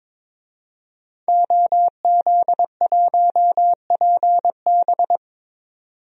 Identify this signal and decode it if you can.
Morse code, heard as a keyed tone, about 22 wpm, 700 Hz. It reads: OZ1PB
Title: DXCC-100 Call Signs - 5 Characters - 22wpm